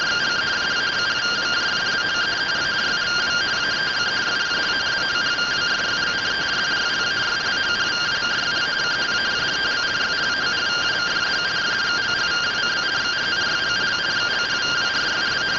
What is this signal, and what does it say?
Two-tone teleprinter signal, about 170 Hz shift.